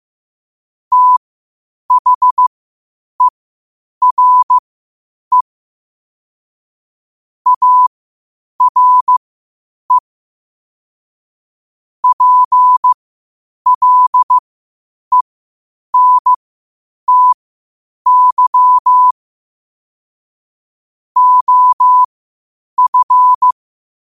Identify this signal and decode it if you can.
Morse code, heard as a keyed tone, about 15 wpm, spaced as for 8 wpm, 1000 Hz. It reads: THERE ARE PLENTY OF
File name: qotdfiles / 8wpm